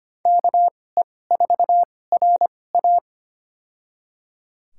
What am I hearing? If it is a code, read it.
Morse code, heard as a keyed tone, about 25 wpm, spaced as for 18 wpm, 700 Hz. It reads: KE4RA
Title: USA Call Signs - 5 Characters - 2x Character Spacing - 25wpm